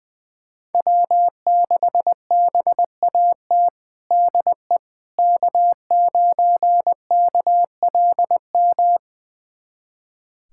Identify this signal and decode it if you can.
Morse code, heard as a keyed tone, about 20 wpm, 700 Hz. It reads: W6BAT DE K9KLM
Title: USA Callsign DE Callsign - 5 Characters - 20wpm